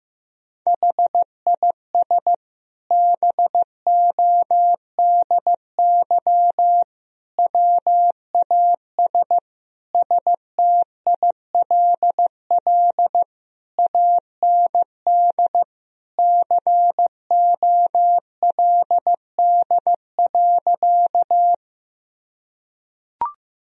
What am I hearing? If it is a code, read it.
Morse code, heard as a keyed tone, about 15 wpm, 700 Hz. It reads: HIS BODY WAS STILL AND COLD.
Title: Sentences from Top 1000 Words - Review Part 2 - 15wpm